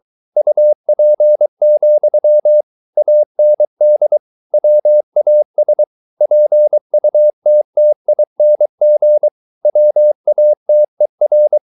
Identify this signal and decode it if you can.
Morse code, heard as a keyed tone, about 23 wpm, 600 Hz. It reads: UP, AND WAS PUTTING WATER
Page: Romanen Never af Ken Follet på engelsk.